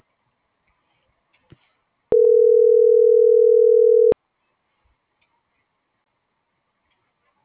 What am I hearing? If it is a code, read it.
Ringback tone (North America), 440 + 480 Hz.